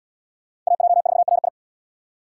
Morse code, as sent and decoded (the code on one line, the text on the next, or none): .- .--- ....- .-. ..
AJ4RI